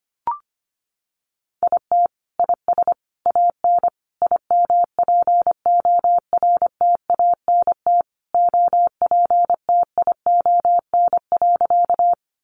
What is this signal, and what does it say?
Morse code, heard as a keyed tone, about 25 wpm, 700 Hz. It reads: IT IS AN IMPORTANT OPTION.
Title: Sentences from Top 1300 Words 25wpm